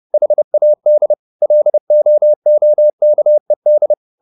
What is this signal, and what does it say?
Morse code, heard as a keyed tone, about 30 wpm, 600 Hz. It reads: HAD LOOKED
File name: the old man and the sea 2 30wpm